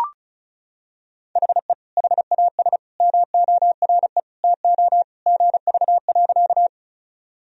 Morse code, heard as a keyed tone, about 35 wpm, 700 Hz. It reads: HE HAS MORE TO GV.
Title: Sentences from Top 200 Words and Abbreviations 35wpm